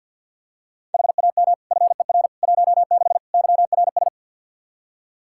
Morse code, heard as a keyed tone, about 46 wpm, 700 Hz. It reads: HAM FER 16 YRS
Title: Practice QSO - Exchange 3 - More Info - 50wpm